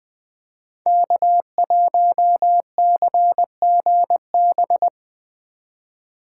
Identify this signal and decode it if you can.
Morse code, heard as a keyed tone, about 20 wpm, 700 Hz. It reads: K1CGB
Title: Code Groups - 5 Letter:Numbers - 20wpm